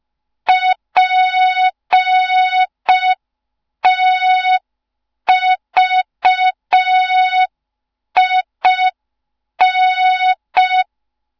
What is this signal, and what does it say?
Morse code, heard as a keyed tone, about 5 wpm, 3700 Hz. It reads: PTVIN
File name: Morze-HA2MN / hang / 07 GYAKORLAT